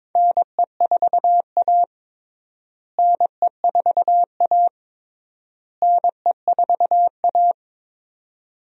Morse code, heard as a keyed tone, about 22 wpm, 700 Hz. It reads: NE4A NE4A NE4A
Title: USA Call Signs - 4 Characters - Repeated 3x - 22wpm